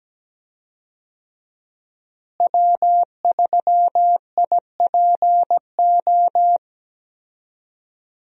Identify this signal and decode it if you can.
Morse code, heard as a keyed tone, about 17 wpm, 700 Hz. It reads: W3IPO
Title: USA Call Signs - 5 Characters - Repeated 3x - 17wpm